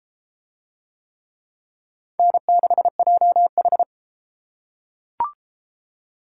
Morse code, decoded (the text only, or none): N6JH